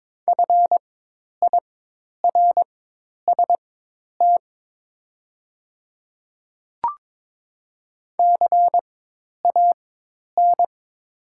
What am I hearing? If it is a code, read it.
Morse code, heard as a keyed tone, about 22 wpm, spaced as for 10 wpm, 700 Hz. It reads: FIRST CAN
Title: 4x Character Spacing - Top 100 Words - 22wpm